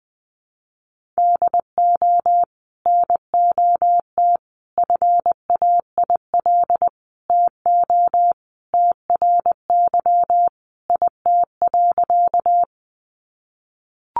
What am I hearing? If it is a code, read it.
Morse code, heard as a keyed tone, about 20 wpm, 700 Hz. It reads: DO NOT FAIL TO TRY IT.
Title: Sentences from Top 1100 Words 20wpm